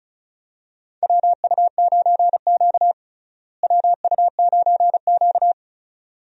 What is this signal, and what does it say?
Morse code, heard as a keyed tone, about 36 wpm, 700 Hz. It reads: WU9Q WU9Q
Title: USA Call Signs - 4 Characters - Repeated 2x - 35wpm